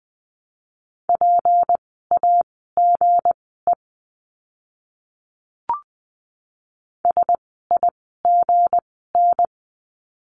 Morse code, heard as a keyed tone, about 20 wpm, spaced as for 15 wpm, 700 Hz. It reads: PAGE SIGN
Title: 160 Lesson 26f - Letter g - Words 20wpm